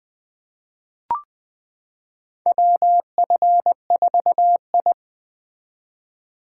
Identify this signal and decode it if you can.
Morse code, heard as a keyed tone, about 20 wpm, 700 Hz. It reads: WF4I
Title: 2023 SOTA Chaser Call Signs - 20wpm